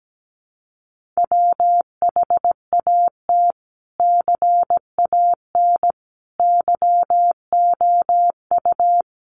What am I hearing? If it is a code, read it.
Morse code, heard as a keyed tone, about 17 wpm, 700 Hz. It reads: WHAT CAN YOU
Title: Sentences from Top 1500 Words 17wpm